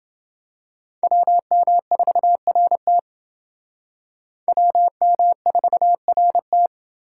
Morse code, heard as a keyed tone, about 30 wpm, 700 Hz. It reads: WM4RT WM4RT